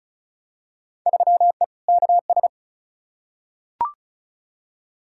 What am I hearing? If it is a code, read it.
Morse code, heard as a keyed tone, about 34 wpm, 700 Hz. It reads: 3E KS
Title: Field Day Exchanges Encore - 35wpm